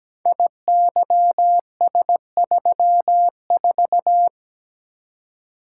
Morse code, heard as a keyed tone, about 17 wpm, 700 Hz. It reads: IYS34
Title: Code Groups - 5 Letter:Numbers - 17wpm